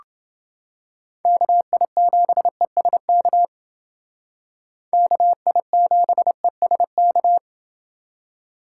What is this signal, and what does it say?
Morse code, heard as a keyed tone, about 29 wpm, 700 Hz. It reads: KI7ESK KI7ESK